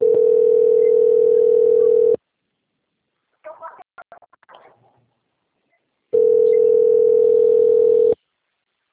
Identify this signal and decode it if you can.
Ringback tone (North America), 440 + 480 Hz.